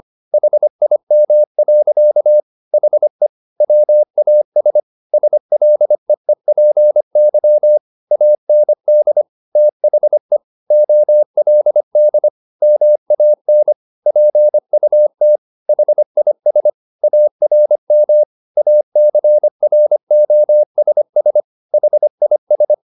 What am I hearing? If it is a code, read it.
Morse code, heard as a keyed tone, about 25 wpm, 600 Hz. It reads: HIM. HE WAS SLEEPY AND THE OLD MAN PUT HIS ARM ACROSS HIS
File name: the old man and the sea part 1 25wpm